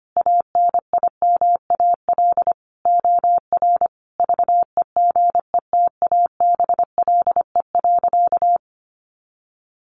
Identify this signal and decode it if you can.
Morse code, heard as a keyed tone, about 25 wpm, 700 Hz. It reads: ANIMAL OR VEGETABLE.
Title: Sentences from Top 1300 Words 25wpm